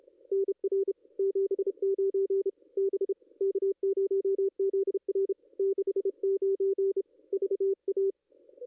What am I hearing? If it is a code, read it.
Morse code, heard as a keyed tone, about 32 wpm, 410 Hz. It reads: NR 79 B K0ZR 69 VA
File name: contests / ss-cw-k5tr-op-n5rz -2200